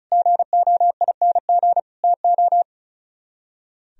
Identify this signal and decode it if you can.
Morse code, heard as a keyed tone, about 43 wpm, 700 Hz. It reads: GOING TO
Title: Speed-Racing - Sets of 4 Words 35wpm